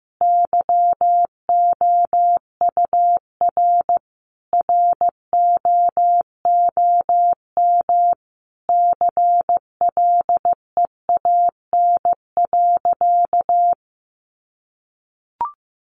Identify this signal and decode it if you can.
Morse code, heard as a keyed tone, about 15 wpm, 700 Hz. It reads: YOUR ROOM CLEAN.
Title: Sentences from Top 1100 Words 15wpm